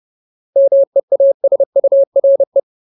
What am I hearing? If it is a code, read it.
Morse code, heard as a keyed tone, about 30 wpm, 550 Hz. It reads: MEASURE
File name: Ultimate Word List / 30WPM Ultimate